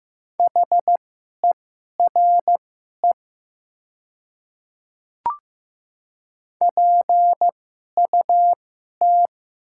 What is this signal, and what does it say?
Morse code, heard as a keyed tone, about 15 wpm, spaced as for 11 wpm, 700 Hz. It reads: HERE PUT
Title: 2x Character Spacing - Top 200 Words - 15wpm